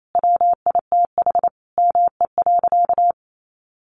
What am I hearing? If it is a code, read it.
Morse code, heard as a keyed tone, about 28 wpm, 700 Hz. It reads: WITH ME.
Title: Sentences from Top 1500 Words 28wpm